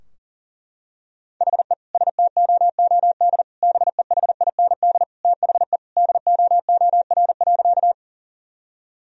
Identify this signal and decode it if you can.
Morse code, heard as a keyed tone, about 38 wpm, 700 Hz. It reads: HE STOOD BEHIND THE DOOR.
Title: Sentences from Top 1000 Words - Review Part 2 - 40wpm